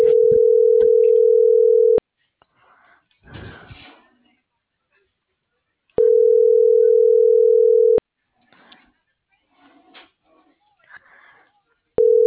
Ringback tone (North America), 440 + 480 Hz.